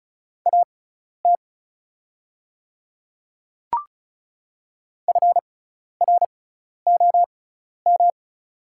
Morse code, heard as a keyed tone, about 36 wpm, spaced as for 12 wpm, 700 Hz. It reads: AT FROM
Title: 6x Character Spacing - Top 100 Words - 35wpm